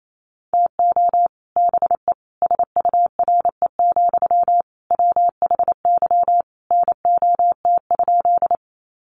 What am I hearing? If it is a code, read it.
Morse code, heard as a keyed tone, about 28 wpm, 700 Hz. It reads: TO BE SURE, WHY NOT?
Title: Sentences from Top 600 Words 28wpm